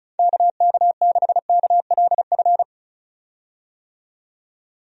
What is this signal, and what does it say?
Morse code, heard as a keyed tone, about 34 wpm, 700 Hz. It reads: KK6KLF
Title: USA Call Signs - 6 Characters - 35wpm